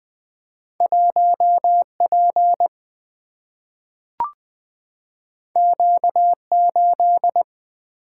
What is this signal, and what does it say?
Morse code, heard as a keyed tone, about 20 wpm, 700 Hz. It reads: 1P Q8
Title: Single-Letter Single-Number 20wpm